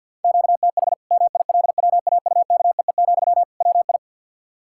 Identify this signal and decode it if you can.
Morse code, heard as a keyed tone, about 52 wpm, 700 Hz. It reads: QTH MILWAUKEE, WI
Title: Practice QSO - Exchange 1 - Beginning - 50wpm